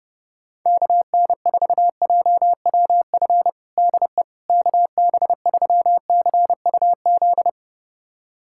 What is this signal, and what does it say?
Morse code, heard as a keyed tone, about 30 wpm, 700 Hz. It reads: KN4JWF DE KB3CUZ
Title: USA Callsign DE Callsign 30wpm